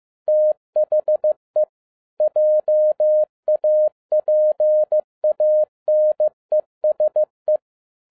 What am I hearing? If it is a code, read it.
Morse code, heard as a keyed tone, about 15 wpm, 610 Hz. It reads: THE JAPANESE
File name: 2011-10-mp3 / n02-FIELD DAY CONTEST QRV FROM SAKAE VILLAGE 15wpm